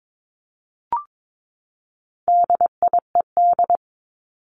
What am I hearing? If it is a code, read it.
Morse code, heard as a keyed tone, about 22 wpm, 700 Hz. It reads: DIED
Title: ED Suffix - Top 100 Words with Base Words 22wpm